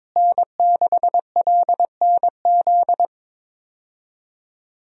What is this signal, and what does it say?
Morse code, heard as a keyed tone, about 22 wpm, 700 Hz. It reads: N6LNZ
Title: USA Call Signs - 5 Characters - Repeated 2x - 22wpm